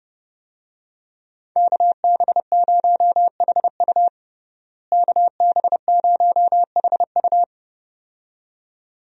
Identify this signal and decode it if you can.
Morse code, heard as a keyed tone, about 32 wpm, 700 Hz. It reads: KB0HU KB0HU
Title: USA Call Signs - 5 Characters - Repeated 2x - 30wpm